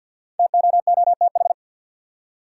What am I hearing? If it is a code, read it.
Morse code, heard as a keyed tone, about 51 wpm, 700 Hz. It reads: TOOTH
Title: Five Letter Words - 50wpm